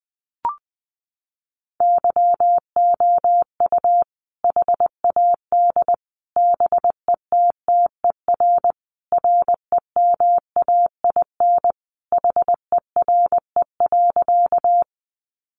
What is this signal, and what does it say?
Morse code, heard as a keyed tone, about 20 wpm, 700 Hz. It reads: YOU HAD BETTER REMAIN HERE.
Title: Sentences from Top 1200 Words 20wpm